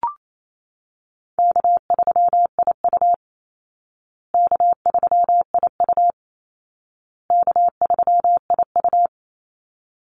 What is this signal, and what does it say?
Morse code, heard as a keyed tone, about 28 wpm, 700 Hz. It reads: K3IU K3IU K3IU
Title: Call Signs Repeated 2x - 28wpm